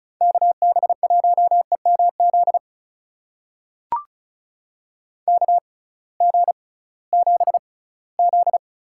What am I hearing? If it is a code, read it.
Morse code, heard as a keyed tone, about 36 wpm, spaced as for 20 wpm, 700 Hz. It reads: KB1EMZ KG7Z
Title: USA Call Signs - 6 Characters - 6x Character Spacing - 35wpm